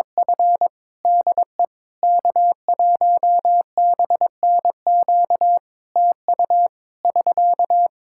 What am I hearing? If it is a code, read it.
Morse code, heard as a keyed tone, about 22 wpm, 700 Hz. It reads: F DE K1BNQ TU <SK>
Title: Practice QSO - Exchange 4 - Ending - 22wpm